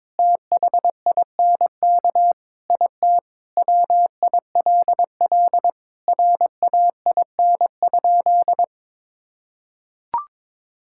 Morse code, heard as a keyed tone, about 22 wpm, 700 Hz. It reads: THINK IT WILL RAIN?
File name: practice / 22wpm / Sentences with 4-Word Phrases - 22wpm